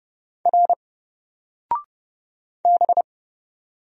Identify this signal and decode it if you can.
Morse code, heard as a keyed tone, about 29 wpm, 700 Hz. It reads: R B
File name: Single-Letter-Number-30wpm